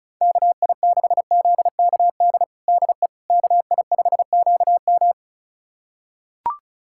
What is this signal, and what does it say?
Morse code, heard as a keyed tone, about 36 wpm, 700 Hz. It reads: KI6ZKD DE KI5QM
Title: USA Callsign DE Callsign 35wpm